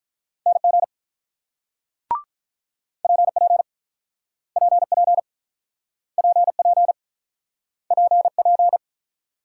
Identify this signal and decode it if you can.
Morse code, heard as a keyed tone, about 48 wpm, 700 Hz. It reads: NG PP PP PP PP